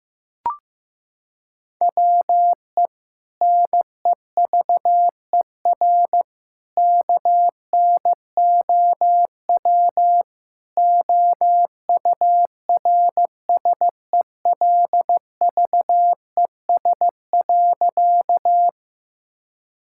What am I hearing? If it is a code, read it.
Morse code, heard as a keyed tone, about 15 wpm, 700 Hz. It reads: WE NEVER KNOW OURSELVES.
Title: Sentences from Top 1600 Words 15wpm